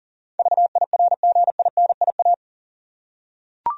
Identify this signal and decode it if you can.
Morse code, heard as a keyed tone, about 40 wpm, 700 Hz. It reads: VIRGINIA
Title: USA Common Names - 40wpm